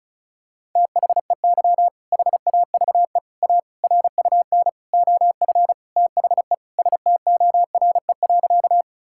Morse code, heard as a keyed tone, about 34 wpm, 700 Hz. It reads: THEY HAVE A RUN OF THE STORE.
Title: Sentences from Top 1000 Words - Review Part 5 - 35wpm